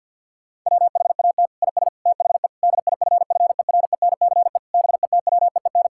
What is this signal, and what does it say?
Morse code, heard as a keyed tone, about 50 wpm, 700 Hz. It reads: WHAT IS THE DIFFERENCE BETWEEN
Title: Sentences from Top 1000 Words - Review Part 5 - 50wpm